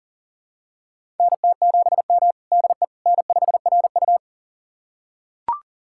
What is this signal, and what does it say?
Morse code, heard as a keyed tone, about 40 wpm, 700 Hz. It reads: NT7M DE N5RU
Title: USA Callsign DE Callsign - 4 Characters - 40wpm